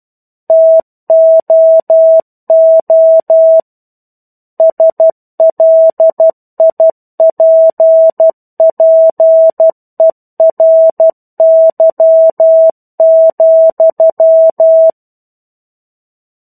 Morse code, heard as a keyed tone, about 12 wpm, 650 Hz. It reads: TOO SLIPPERY,
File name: 01-Chapter-01 12 WPM